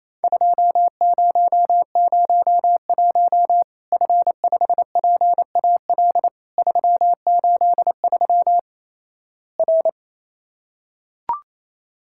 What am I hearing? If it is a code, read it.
Morse code, heard as a keyed tone, about 28 wpm, 700 Hz. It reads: 2001 F5PAL 383 R
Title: Worked All Europe DX Contest - QTC Exchange - 28wpm